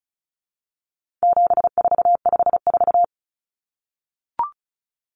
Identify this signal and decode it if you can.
Morse code, heard as a keyed tone, about 34 wpm, 700 Hz. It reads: 7454
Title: Numbers - 4-Digit 35wpm